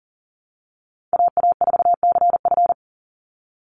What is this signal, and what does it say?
Morse code, heard as a keyed tone, about 40 wpm, 700 Hz. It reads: AA4CF
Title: Call Signs Repeated 2x - 40wpm